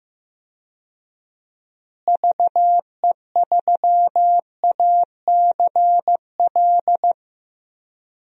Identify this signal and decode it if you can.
Morse code, heard as a keyed tone, about 15 wpm, 700 Hz. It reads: VE3ACL
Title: DXCC-100 Call Signs - 6 Characters - Repeated 3x - 15wpm